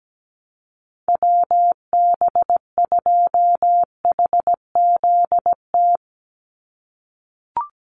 Morse code, heard as a keyed tone, about 17 wpm, 700 Hz. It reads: WB2HZT
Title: USA Call Signs 17wpm